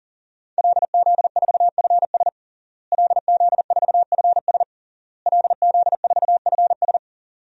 Morse code, heard as a keyed tone, about 40 wpm, 700 Hz. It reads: LZ4FS LZ4FS LZ4FS
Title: DXCC-100 Call Signs - 5 Characters - Repeated 3x - 40wpm